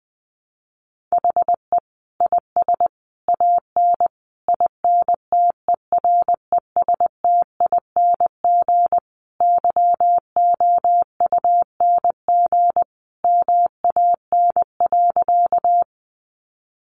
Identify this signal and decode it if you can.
Morse code, heard as a keyed tone, about 20 wpm, 700 Hz. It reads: HE IS AN INTERESTING YOUNG MAN.
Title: Sentences from Top 1000 Words 20wpm